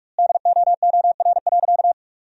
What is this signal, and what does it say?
Morse code, heard as a keyed tone, about 46 wpm, 700 Hz. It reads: DOOR.
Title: 2x Word-Spacing - Sentences from Top 400 Words - 45wpm